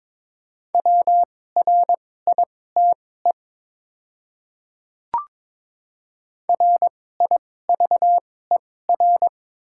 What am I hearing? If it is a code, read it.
Morse code, heard as a keyed tone, about 22 wpm, spaced as for 16 wpm, 700 Hz. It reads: WRITE RIVER
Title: 2x Character Spacing - Top 300 Words - 22wpm